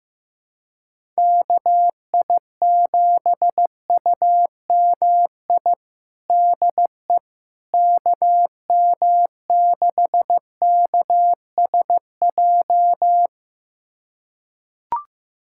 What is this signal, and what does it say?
Morse code, heard as a keyed tone, about 16 wpm, 700 Hz. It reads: KI7UMI DE KM6KSJ
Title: USA Callsign DE Callsign - 6 Characters - 15wpm